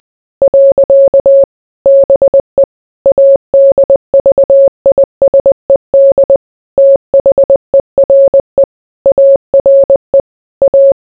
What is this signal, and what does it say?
Morse code, heard as a keyed tone, about 20 wpm, 550 Hz. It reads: . BE ADVISED THERE ARE A